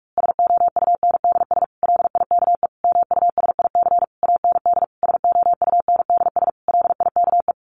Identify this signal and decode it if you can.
Morse code, heard as a keyed tone, about 44 wpm, 700 Hz. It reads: SOUNDS LIKE MUSIC AND SOUNDS LIKE
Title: A Sound Of Thunder 45wpm